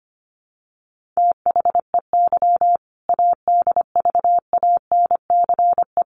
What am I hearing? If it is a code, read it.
Morse code, heard as a keyed tone, about 25 wpm, 700 Hz. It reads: THEY ADVANCE
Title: Sentences from Top 1400 Words 25wpm